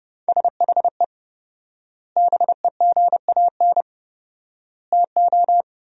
Morse code, heard as a keyed tone, about 30 wpm, 700 Hz. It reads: SHE BEGAN TO
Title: 4x Word-Spacing - Sentences from Top 500 Words - 30wpm